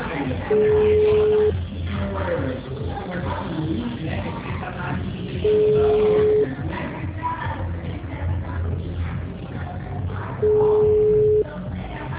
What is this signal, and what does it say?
Ringback tone (Europe, 425 Hz), with speech.
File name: out- -2008- - - .4592